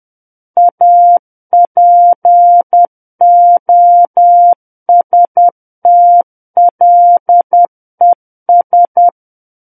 Morse code, heard as a keyed tone, about 10 wpm, 700 Hz. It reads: APOSTLES
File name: Book One / 01-Chapter 1 - 10 WPM